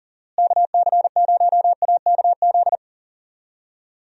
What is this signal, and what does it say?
Morse code, heard as a keyed tone, about 39 wpm, 700 Hz. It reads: KC0AKZ
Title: USA Call Signs - 6 Characters - Repeated 2x - 40wpm